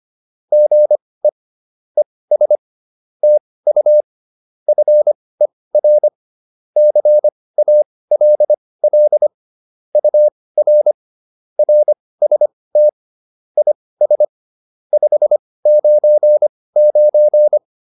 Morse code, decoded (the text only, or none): GE ES TU FER CALL UR RST IS 599